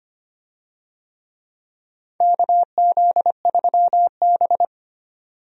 Morse code, heard as a keyed tone, about 25 wpm, 700 Hz. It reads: KZ3B
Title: USA Call Signs - 4 Characters - 25wpm